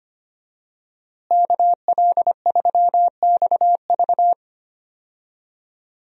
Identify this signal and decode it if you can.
Morse code, heard as a keyed tone, about 25 wpm, 700 Hz. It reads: KL3XV
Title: USA Call Signs - 5 Characters - Repeated 3x - 25wpm